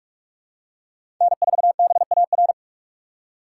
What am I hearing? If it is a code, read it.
Morse code, heard as a keyed tone, about 46 wpm, 700 Hz. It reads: N4BAR
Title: USA Call Signs - 6x Character Spacing - 45wpm